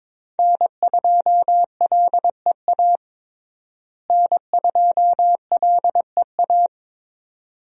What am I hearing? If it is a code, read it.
Morse code, heard as a keyed tone, about 22 wpm, 700 Hz. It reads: N2LEA N2LEA
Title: USA Call Signs - 5 Characters - Repeated 2x - 22wpm